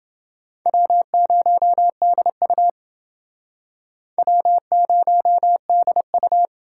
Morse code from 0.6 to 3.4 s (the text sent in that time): W0DU